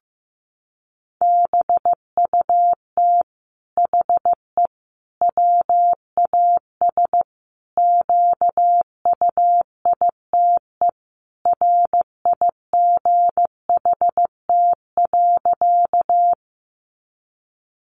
Morse code, heard as a keyed tone, about 15 wpm, 700 Hz. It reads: BUT HE WAS QUITE RIGHT.
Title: Sentences from Top 700 Words 15wpm